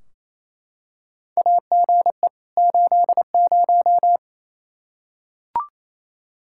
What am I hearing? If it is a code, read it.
Morse code, heard as a keyed tone, about 27 wpm, 700 Hz. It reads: AGE 80
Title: Practice QSO - Exchange 3 - More Info - 28wpm